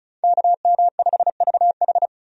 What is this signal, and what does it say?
Morse code, heard as a keyed tone, about 36 wpm, 700 Hz. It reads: KM5VH
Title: USA Call Signs - 5 Characters - 35wpm